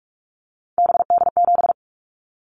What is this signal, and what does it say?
Morse code, heard as a keyed tone, about 46 wpm, 700 Hz. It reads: BD7
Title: Callsign Prefixes for Top 100 DXCC - 45wpm